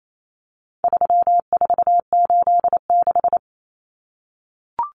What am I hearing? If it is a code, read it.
Morse code, heard as a keyed tone, about 28 wpm, 700 Hz. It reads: 3486
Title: Common Cut Numbers - 4-Digit 28wpm